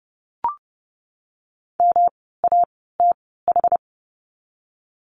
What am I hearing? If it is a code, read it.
Morse code, heard as a keyed tone, about 32 wpm, spaced as for 17 wpm, 700 Hz. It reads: MATH
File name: course / 30:10wpm / 108 Lesson 18f - Letter m - Words 30wpm